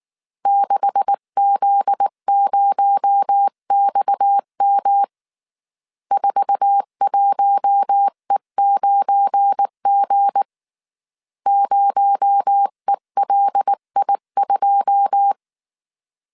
Morse code, heard as a keyed tone, about 19 wpm, 800 Hz. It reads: 6Z0XM 41E9G 0ELI2
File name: CW / DET / 1 9 1500C S C 18wpm 2539